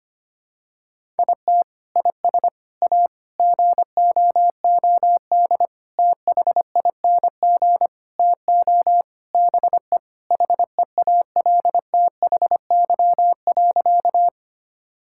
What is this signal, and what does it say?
Morse code, heard as a keyed tone, about 25 wpm, 700 Hz. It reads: IT IS A GOOD THING TO BE HEALTHY.
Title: Sentences from Top 1000 Words - Review Part 5 - 25wpm